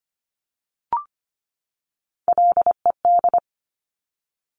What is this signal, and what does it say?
Morse code, heard as a keyed tone, about 25 wpm, 700 Hz. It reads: LED
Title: Top Three Letter Combinations in Words 25wpm